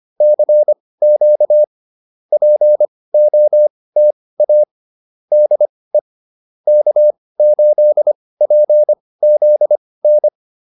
Morse code, heard as a keyed tone, about 25 wpm, spaced as for 18 wpm, 600 Hz. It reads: CQ POTA DE K8PZN
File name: POTA201 18